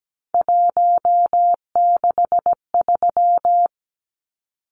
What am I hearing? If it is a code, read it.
Morse code, heard as a keyed tone, about 17 wpm, 700 Hz. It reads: 163
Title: Numbers - 3-Digit 17wpm